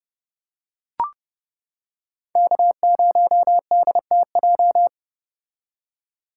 Morse code, heard as a keyed tone, about 29 wpm, 700 Hz. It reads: K0DTJ